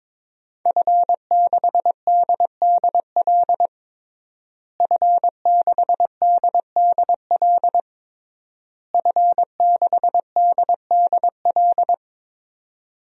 Morse code, heard as a keyed tone, about 22 wpm, 700 Hz. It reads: F6DDL F6DDL F6DDL
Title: DXCC-100 Call Signs - 5 Characters - Repeated 3x - 22wpm